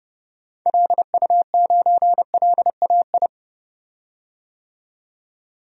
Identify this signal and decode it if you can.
Morse code, heard as a keyed tone, about 32 wpm, 700 Hz. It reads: LU9LAI
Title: DXCC-100 Call Signs - 6 Characters - 30wpm